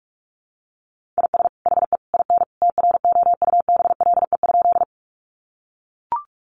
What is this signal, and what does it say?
Morse code, heard as a keyed tone, about 44 wpm, 700 Hz. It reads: IS HE IN TROUBLE?
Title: Sentences from Top 1200 Words 45wpm